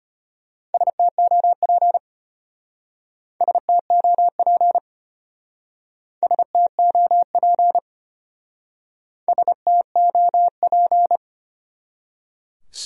Morse code, heard as a keyed tone, about 31 wpm, 700 Hz. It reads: STOP STOP STOP STOP
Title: Speed-Racing Top 500 Words 25wpm